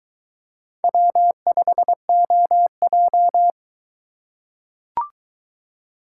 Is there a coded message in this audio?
Yes: Morse code from 0.8 to 3.5 s, heard as a keyed tone, about 22 wpm, 700 Hz. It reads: W5OJ